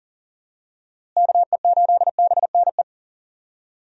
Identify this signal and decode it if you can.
Morse code, heard as a keyed tone, about 40 wpm, 700 Hz. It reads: KE8BNE
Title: USA Call Signs - 6 Characters - Repeated 2x - 40wpm